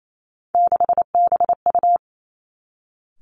Morse code, heard as a keyed tone, about 28 wpm, 700 Hz. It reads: 6BU
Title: Three Challenging Letter-Number Combinations 28wpm